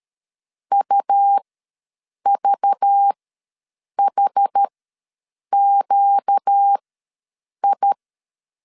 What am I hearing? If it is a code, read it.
Morse code, heard as a keyed tone, about 13 wpm, spaced as for 7 wpm, 800 Hz. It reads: UVHQI
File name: LEZIONI / 5 LBVC / 5 LBV ws 12wpm 3015